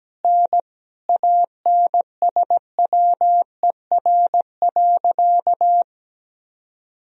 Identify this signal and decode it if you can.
Morse code, heard as a keyed tone, about 17 wpm, 700 Hz. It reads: N ANSWER.
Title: Sentences from Top 500 Words and Abbreviations 17wpm